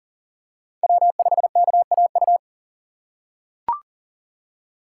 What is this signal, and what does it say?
Morse code, heard as a keyed tone, about 39 wpm, 700 Hz. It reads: W5KAU
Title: USA Call Signs - 5 Characters - 4x Character Spacing - 40wpm